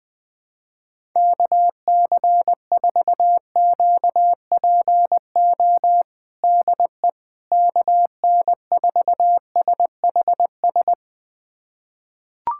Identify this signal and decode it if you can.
Morse code, heard as a keyed tone, about 20 wpm, 700 Hz. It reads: KC4QPO DE KN4SHS
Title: USA Callsign DE Callsign 20wpm